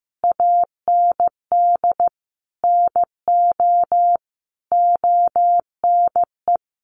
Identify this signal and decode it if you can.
Morse code, heard as a keyed tone, about 15 wpm, 700 Hz. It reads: AND NO ONE
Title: Sentences from Top 1400 Words 15wpm